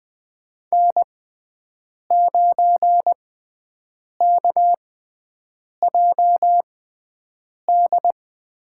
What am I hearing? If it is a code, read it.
Morse code, heard as a keyed tone, about 20 wpm, spaced as for 7 wpm, 700 Hz. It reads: N9KJD
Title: USA Call Signs - 6x Character Spacing - 20wpm